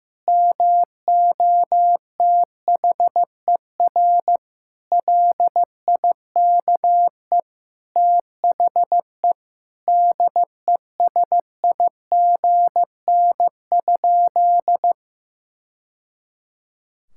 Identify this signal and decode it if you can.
Morse code, heard as a keyed tone, about 15 wpm, 700 Hz. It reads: MOTHER LIKE THE DESIGN?
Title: Sentences from Top 1000 Words - Review Part 5 - 15wpm